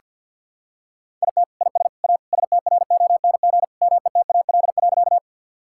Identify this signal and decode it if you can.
Morse code, heard as a keyed tone, about 46 wpm, 700 Hz. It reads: IT IS A STRONG METAL.
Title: Sentences from Top 1000 Words - Review Part 6 - 50wpm